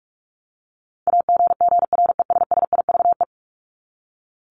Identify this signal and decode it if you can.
Morse code, heard as a keyed tone, about 44 wpm, 700 Hz. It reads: AGGRESSIVE
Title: Top 1501-1750 Words 45wpm